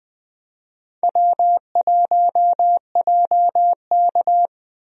Morse code, heard as a keyed tone, about 20 wpm, 700 Hz. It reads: W1JK
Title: USA Call Signs - 4 Characters - Repeated 3x - 20wpm